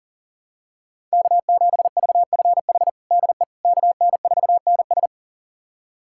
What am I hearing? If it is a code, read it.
Morse code, heard as a keyed tone, about 40 wpm, 700 Hz. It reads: K7VFH DE KN4NS
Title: USA Callsign DE Callsign - 5 Characters - 40wpm